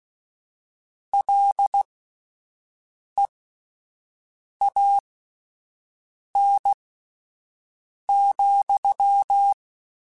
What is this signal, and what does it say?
Morse code, heard as a keyed tone, about 16 wpm, spaced as for 5 wpm, 800 Hz. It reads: LEAN,